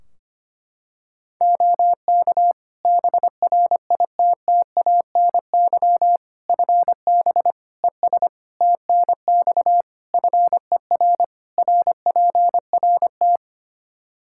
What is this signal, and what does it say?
Morse code, heard as a keyed tone, about 25 wpm, 700 Hz. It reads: OK BRITTANY FB ES TNX FER RPRT
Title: K1USN SST QSO Practice 25wpm